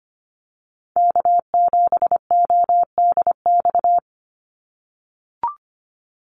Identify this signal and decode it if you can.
Morse code, heard as a keyed tone, about 25 wpm, 700 Hz. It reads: K7ODX
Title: NA SOTA Activator Callsigns 25wpm